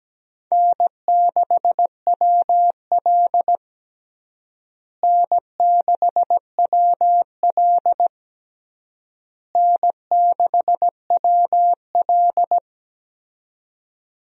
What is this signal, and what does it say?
Morse code, heard as a keyed tone, about 17 wpm, 700 Hz. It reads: N6WL N6WL N6WL
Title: USA Call Signs - 4 Characters - Repeated 3x - 17wpm